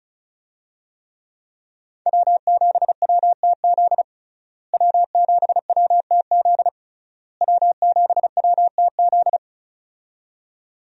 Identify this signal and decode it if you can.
Morse code, heard as a keyed tone, about 36 wpm, 700 Hz. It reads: W7WTZ W7WTZ W7WTZ
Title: USA Call Signs - 5 Characters - Repeated 3x - 35wpm